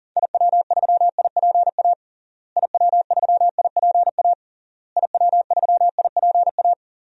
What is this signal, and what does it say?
Morse code, heard as a keyed tone, about 37 wpm, 700 Hz. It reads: IW3IPA IW3IPA IW3IPA
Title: DXCC-100 Call Signs - 6 Characters - Repeated 3x - 40wpm